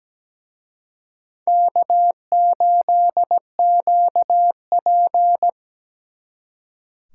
Morse code, heard as a keyed tone, about 20 wpm, 700 Hz. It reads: K8QP
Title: Speed Racing - USA 1x2 Call Signs 17wpm